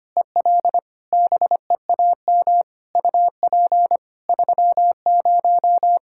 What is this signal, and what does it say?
Morse code, heard as a keyed tone, about 25 wpm, 700 Hz. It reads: EL BEAM UP 30
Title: Practice QSO - Exchange 2 - Info - 25wpm